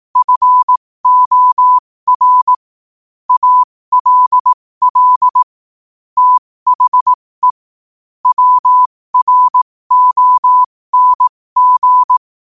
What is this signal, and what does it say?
Morse code, heard as a keyed tone, about 18 wpm, spaced as for 15 wpm, 1000 Hz. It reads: FOR ALL THE WRONG